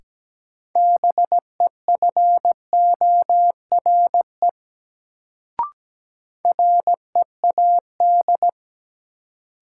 Morse code, heard as a keyed tone, about 17 wpm, 700 Hz. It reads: BEFORE READ
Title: Sentences from Top 200 Words - New Word Drill - 17wpm